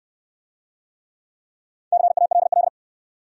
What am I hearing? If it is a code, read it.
Morse code, heard as a keyed tone, about 69 wpm, 700 Hz. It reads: CALL